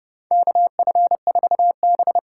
Morse code, heard as a keyed tone, about 30 wpm, 700 Hz. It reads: KF4B
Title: USA Call Signs - 8x Character Spacing - 30wpm